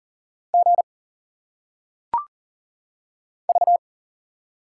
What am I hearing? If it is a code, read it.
Morse code, heard as a keyed tone, about 40 wpm, 700 Hz. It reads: G V
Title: Single Letters - Phonetic Pronunciation 40wpm